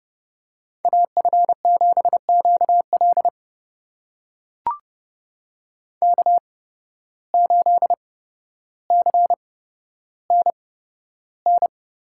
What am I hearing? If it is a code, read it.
Morse code, heard as a keyed tone, about 32 wpm, spaced as for 13 wpm, 700 Hz. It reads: AF7QL K8CNN
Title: USA Call Signs - 5 Characters - 8x Character Spacing - 30wpm